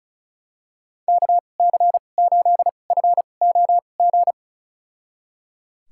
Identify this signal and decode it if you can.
Morse code, heard as a keyed tone, about 34 wpm, spaced as for 25 wpm, 700 Hz. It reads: KC8FOG
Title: USA Call Signs - 6 Characters - 2x Character Spacing - 35wpm